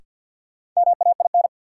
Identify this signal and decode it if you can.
Morse code, heard as a keyed tone, about 53 wpm, 700 Hz. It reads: MAIN
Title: Sentences from Top 1300 Words - New Word Drill Alt - 50wpm